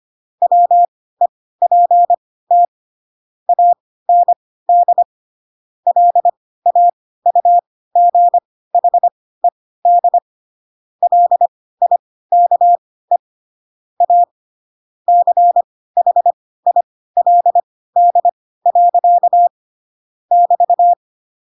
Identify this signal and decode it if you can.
Morse code, heard as a keyed tone, about 25 wpm, spaced as for 16 wpm, 700 Hz. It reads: WEPT AND LAUGHED LIKE A CHILD. <BT>
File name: Pinocchio / Pinocchio ch1 10 25 ik1hge